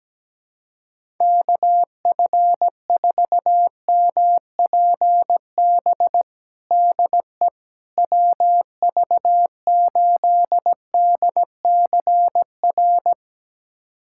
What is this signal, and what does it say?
Morse code, heard as a keyed tone, about 17 wpm, 700 Hz. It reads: KF4MPB DE WV8DCR
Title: USA Callsign DE Callsign 17wpm